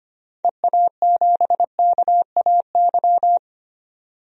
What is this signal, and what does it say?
Morse code, heard as a keyed tone, about 25 wpm, 700 Hz. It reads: EA7KAY
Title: DXCC-100 Call Signs - 6 Characters - 25wpm